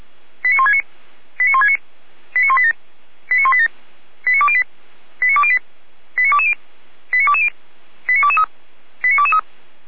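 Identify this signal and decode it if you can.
Five-tone selective call (ZVEI), heard as quick runs of five tones, tones of about 70 ms. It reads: 89169 89169 89187 89187 89208 89208 89220 89220 89223 89223
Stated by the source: ZVEI five-tone selective call received on an SDR.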